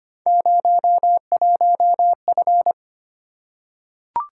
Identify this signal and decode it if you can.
Morse code, heard as a keyed tone, about 25 wpm, 700 Hz. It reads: 01F
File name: Three Letter-Number Combinations - 25wpm